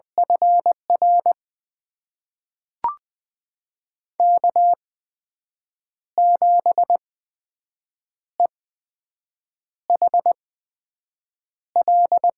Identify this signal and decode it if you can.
Morse code, heard as a keyed tone, about 20 wpm, spaced as for 6 wpm, 700 Hz. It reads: FR K7EHL
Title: USA Call Signs - 8x Character Spacing - 20wpm